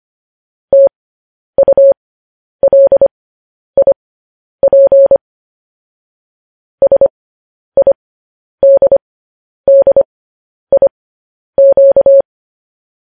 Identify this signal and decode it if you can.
Morse code, heard as a keyed tone, about 25 wpm, spaced as for 10 wpm, 550 Hz. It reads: TULIP SIDDIQ
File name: podcast / News headlines-1x-25x10